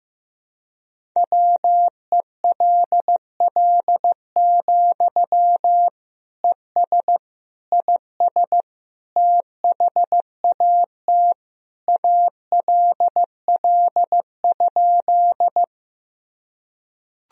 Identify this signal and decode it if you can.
Morse code, heard as a keyed tone, about 16 wpm, 700 Hz. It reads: WELL, ES IS THAT ALL?
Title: Sentences from Top 200 Words and Abbreviations 15wpm